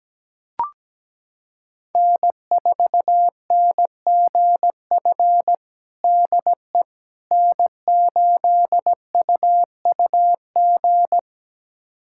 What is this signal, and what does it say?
Morse code, heard as a keyed tone, about 17 wpm, 700 Hz. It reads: N4NGF DE N8UUG
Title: USA Callsign DE Callsign - 5 Characters - 17wpm